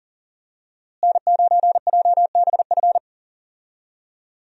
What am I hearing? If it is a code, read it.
Morse code, heard as a keyed tone, about 40 wpm, 700 Hz. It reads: N9JBF
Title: USA Call Signs - 5 Characters - Repeated 2x - 40wpm